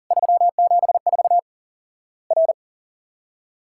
Morse code, heard as a keyed tone, about 40 wpm, 700 Hz. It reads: 374 R
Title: Worked All Europe DX Contest - QTC Exchange - 40wpm